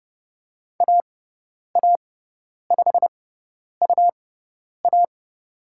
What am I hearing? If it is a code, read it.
Morse code, heard as a keyed tone, about 30 wpm, spaced as for 10 wpm, 700 Hz. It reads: AA5UA
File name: Lesson 18g - Letter m - Call Signs 30wpm